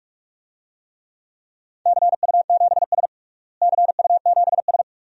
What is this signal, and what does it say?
Morse code, heard as a keyed tone, about 46 wpm, 700 Hz. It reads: CU7S CU7S
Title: DXCC-100 Call Signs - 4 Characters - Repeated 2x - 45wpm